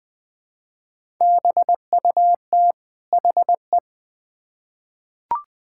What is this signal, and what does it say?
Morse code, heard as a keyed tone, about 20 wpm, 700 Hz. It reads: BUT HE
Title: Build A Phrase - 2-Word Phrases - 20wpm